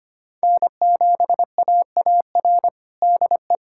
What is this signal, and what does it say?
Morse code, heard as a keyed tone, about 25 wpm, 700 Hz. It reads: N7AAR DE
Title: USA Callsign DE Callsign - 5 Characters - 25wpm